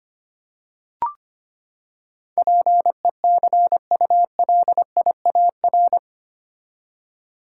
Morse code, heard as a keyed tone, about 25 wpm, 700 Hz. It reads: PECULIAR
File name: Eight Letter Words - 25wpm